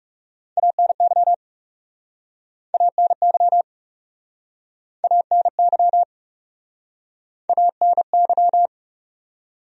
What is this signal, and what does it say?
Morse code, heard as a keyed tone, about 41 wpm, 700 Hz. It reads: ANY ANY ANY ANY